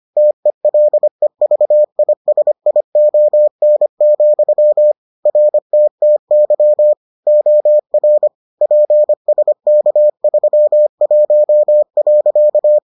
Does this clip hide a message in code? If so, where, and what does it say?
Yes: Morse code from 0.2 to 12.8 s, heard as a keyed tone, about 25 wpm, 600 Hz. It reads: TELEVISION, RTTY OR PSK31.